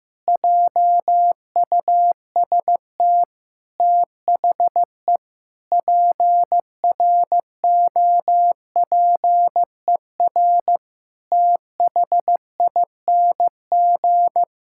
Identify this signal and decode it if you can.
Morse code, heard as a keyed tone, about 16 wpm, 700 Hz. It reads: JUST THE PROPER THING
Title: Sentences from Top 1000 Words - Review Part 5 - 15wpm